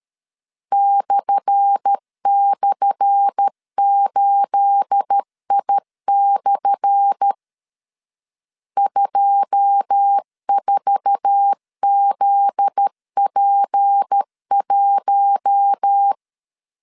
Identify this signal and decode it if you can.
Morse code, heard as a keyed tone, about 13 wpm, 800 Hz. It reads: //8I/ 24ZP1
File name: LEZIONI / 10 SIGNS / 10A SLASH 12wpm 2314